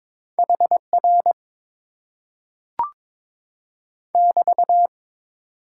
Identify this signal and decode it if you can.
Morse code, heard as a keyed tone, about 22 wpm, 700 Hz. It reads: HR <BT>
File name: Top 90 QSO Elements - 8x Character Spacing - 22wpm